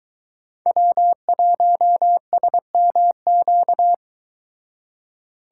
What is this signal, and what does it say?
Morse code, heard as a keyed tone, about 22 wpm, 700 Hz. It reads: W1SMQ